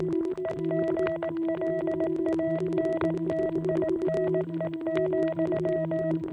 Morse code, heard as a keyed tone, about 18 wpm, 660 Hz. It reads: EGLANTINE,